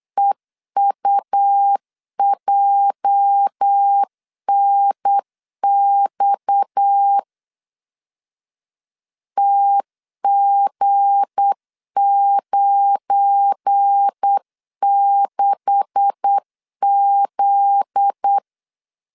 Morse code, heard as a keyed tone, about 8 wpm, 800 Hz. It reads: EUJNX TG96Z
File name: CW / DET / 1 9 1500C S A 8wpm 5637